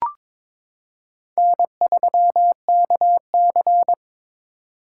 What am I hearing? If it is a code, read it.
Morse code, heard as a keyed tone, about 21 wpm, 700 Hz. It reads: N3KC
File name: practice / 22wpm / Four Letter-Number Combinations - 22wpm